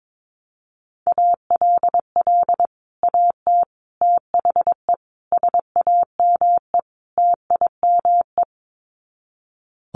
Morse code, heard as a keyed tone, about 22 wpm, 700 Hz. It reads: ALL AT THE SAME TIME
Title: Sets of 5 Words Encore2 22wpm